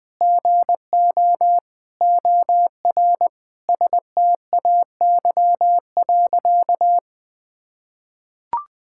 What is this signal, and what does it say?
Morse code, heard as a keyed tone, about 20 wpm, 700 Hz. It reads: GO OR STAY.
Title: Sentences with 5-Word Phrases - 20wpm